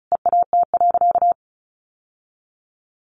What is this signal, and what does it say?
Morse code, heard as a keyed tone, about 34 wpm, 700 Hz. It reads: EAT.
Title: Sentences from Top 600 Words 35wpm